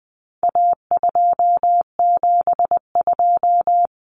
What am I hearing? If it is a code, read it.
Morse code, heard as a keyed tone, about 20 wpm, 700 Hz. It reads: A272
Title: Common Cut Numbers - 4-Digit 20wpm